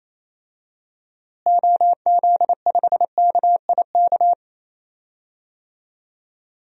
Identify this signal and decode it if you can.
Morse code, heard as a keyed tone, about 28 wpm, 700 Hz. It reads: OZ5KIK
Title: DXCC-100 Call Signs - 6 Characters - Repeated 2x - 28wpm